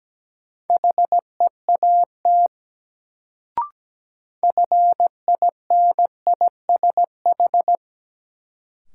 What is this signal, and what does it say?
Morse code, heard as a keyed tone, about 17 wpm, 700 Hz. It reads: HEAT FINISH
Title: Sentences from Top 1000 Words - Review Part 3 - New Word Drill - 17wpm